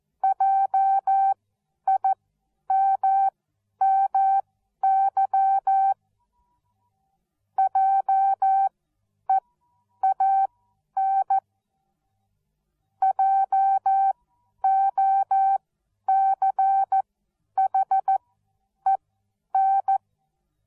Morse code, heard as a keyed tone, about 14 wpm, spaced as for 10 wpm, 780 Hz. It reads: JIMMY JEAN JOCHEN